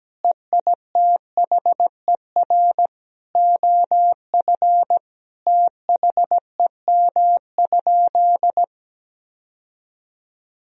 Morse code, heard as a keyed tone, about 17 wpm, 700 Hz. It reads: EITHER OF THEM?
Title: Sentences from Top 1000 Words - Review Part 4 - 17wpm